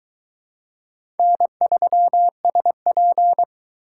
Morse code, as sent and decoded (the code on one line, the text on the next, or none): -. ...-- ... .--.
N3SP